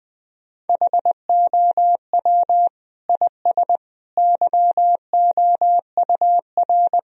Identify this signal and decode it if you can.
Morse code, heard as a keyed tone, about 20 wpm, 700 Hz. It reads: HOW IS YOUR
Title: Sentences from Top 1000 Words - Review Part 4 - 20wpm